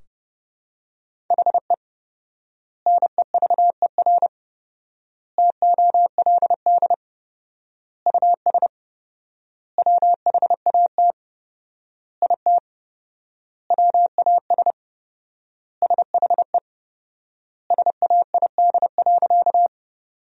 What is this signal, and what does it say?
Morse code, heard as a keyed tone, about 31 wpm, 700 Hz. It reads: HE NEVER TOLD US WHAT IT WAS SHE SAID.
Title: 4x Word-Spacing - Sentences from Top 400 Words - 30wpm